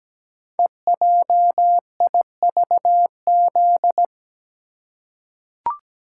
Morse code, heard as a keyed tone, about 17 wpm, 700 Hz. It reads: EJIVZ
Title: Code Groups - 5 Letters - 17wpm